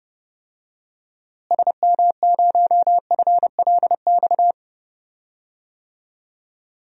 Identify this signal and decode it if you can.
Morse code, heard as a keyed tone, about 30 wpm, 700 Hz. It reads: SM0FLX
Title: DXCC-100 Call Signs - 6 Characters - Repeated 3x - 30wpm